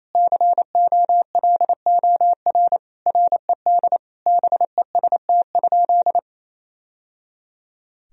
Morse code, heard as a keyed tone, about 28 wpm, 700 Hz. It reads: COLOR RED BEST?
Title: Sentences from Top 1000 Words - Review Part 2 - 28wpm